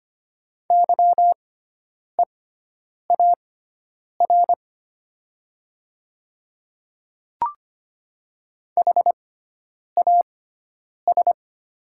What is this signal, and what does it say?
Morse code, heard as a keyed tone, about 25 wpm, spaced as for 9 wpm, 700 Hz. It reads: YEAR HAS
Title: 6x Character Spacing - Top 200 Words - 25wpm